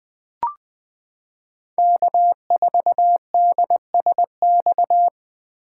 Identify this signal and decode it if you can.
Morse code, heard as a keyed tone, about 20 wpm, 700 Hz. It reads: K4DSX
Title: 2023 POTA Activator Call Signs - 20wpm